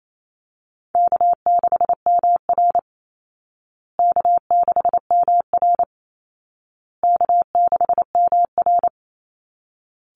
Morse code, heard as a keyed tone, about 28 wpm, 700 Hz. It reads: K6MR K6MR K6MR
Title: Easy Call Signs Repeated 3x - 28wpm